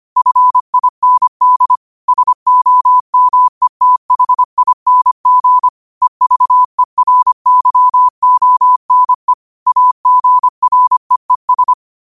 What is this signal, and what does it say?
Morse code, heard as a keyed tone, about 25 wpm, 1000 Hz. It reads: FIND SOMETHING EVERYONE AGREES